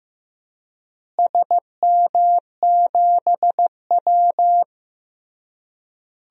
Morse code, heard as a keyed tone, about 16 wpm, 700 Hz. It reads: SM7W
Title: DXCC-100 Call Signs - 4 Characters - Repeated 3x - 15wpm